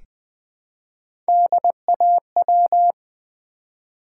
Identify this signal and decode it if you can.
Morse code, heard as a keyed tone, about 20 wpm, 700 Hz. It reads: DAW
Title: Build A Word - 1 Letter At A Time - 4-Letter Words - 20wpm